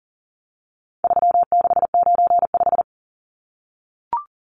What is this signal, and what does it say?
Morse code, heard as a keyed tone, about 37 wpm, 700 Hz. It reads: 3695
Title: Numbers - 4-Digit 40wpm